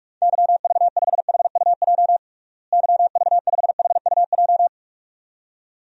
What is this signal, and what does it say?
Morse code, heard as a keyed tone, about 46 wpm, 700 Hz. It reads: YV5HUJ YV5HUJ
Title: DXCC-100 Call Signs - 6 Characters - Repeated 2x - 45wpm